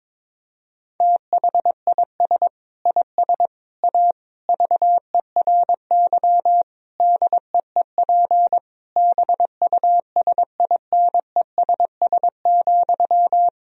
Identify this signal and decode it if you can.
Morse code, heard as a keyed tone, about 22 wpm, 700 Hz. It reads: THIS IS A VERY DEEP BUSINESS,
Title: Sentences from Top 1000 Words - Review Part 3 - 22wpm